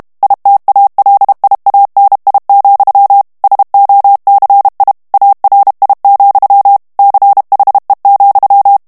Morse code, heard as a keyed tone, about 32 wpm, 800 Hz. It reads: ITALIANI, SOCI ARI, CHE,
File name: texte clair 2 vitesse 30 mots minute